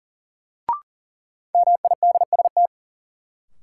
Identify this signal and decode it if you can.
Morse code, heard as a keyed tone, about 37 wpm, 700 Hz. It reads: MIDST
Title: Sets of 4 Words - New Word Drill - 40wpm